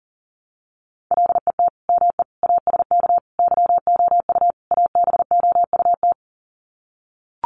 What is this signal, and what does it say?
Morse code, heard as a keyed tone, about 40 wpm, 700 Hz. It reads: LET ME ASK YOU ABOUT
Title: Sets of 5 Words 40wpm